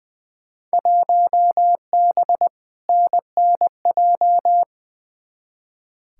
Morse code, heard as a keyed tone, about 20 wpm, 700 Hz. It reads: 1B NNJ
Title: Field Day Exchanges Encore - 20wpm